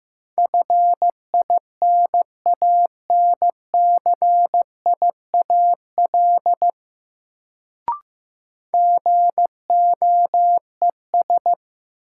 Morse code, heard as a keyed tone, about 15 wpm, 700 Hz. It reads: FINANCIAL GOES
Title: Sentences from Top 1000 Words - Review Part 5 - New Word Drill - 15wpm